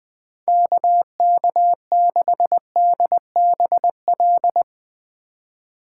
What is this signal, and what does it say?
Morse code, heard as a keyed tone, about 20 wpm, 700 Hz. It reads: KK6DBL
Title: Speed Racing - USA 2x3 Call Signs 17wpm